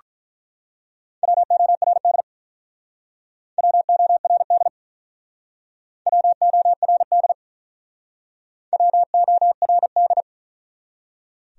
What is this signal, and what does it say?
Morse code, heard as a keyed tone, about 48 wpm, 700 Hz. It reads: WORD WORD WORD WORD